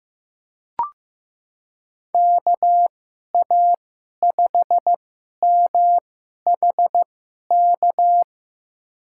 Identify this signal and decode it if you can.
Morse code, heard as a keyed tone, about 15 wpm, spaced as for 11 wpm, 700 Hz. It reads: KA5MHK
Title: USA Call Signs - 6 Characters - 2x Character Spacing - 15wpm